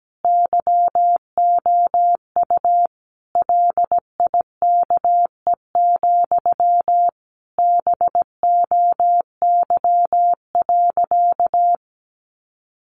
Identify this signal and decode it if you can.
Morse code, heard as a keyed tone, about 17 wpm, 700 Hz. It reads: YOU LIKE, BOY.
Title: Sentences from Top 1400 Words 17wpm